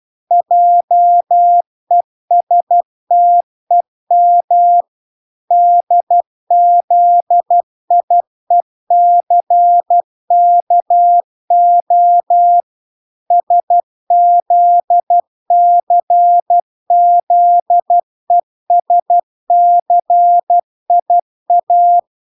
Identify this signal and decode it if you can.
Morse code, heard as a keyed tone, about 12 wpm, 700 Hz. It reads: JESTEM DZIECKO SZCZESCIA
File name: recordings / 12wpm0000